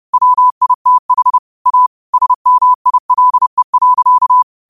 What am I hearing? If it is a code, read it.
Morse code, heard as a keyed tone, about 30 wpm, 1000 Hz. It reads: WITH A SMILE.
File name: qotdfiles / 30wpm